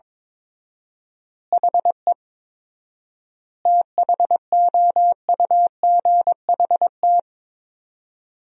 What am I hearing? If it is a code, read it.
Morse code, heard as a keyed tone, about 22 wpm, 700 Hz. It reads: HE THOUGHT
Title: 4x Word-Spacing - Sentences from Top 300 Words - 22wpm